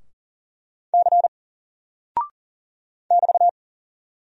Morse code, heard as a keyed tone, about 41 wpm, 700 Hz. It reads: C <BT>
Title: Single Character (Letter:Number:Punctuation:Prosign) Alt 40wpm